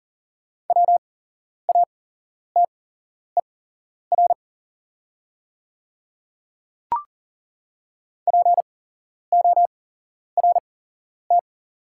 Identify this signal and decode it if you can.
Morse code, heard as a keyed tone, about 40 wpm, spaced as for 11 wpm, 700 Hz. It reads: WATER PORT
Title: 8x Character Spacing - Top 200 Words - 40wpm